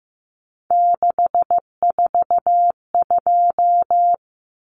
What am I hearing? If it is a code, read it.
Morse code, heard as a keyed tone, about 15 wpm, 700 Hz. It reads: 642
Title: Numbers - 3-Digit 15wpm